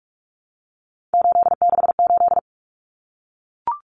Morse code, heard as a keyed tone, about 44 wpm, 700 Hz. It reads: 868
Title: Numbers - 3-Digit 45wpm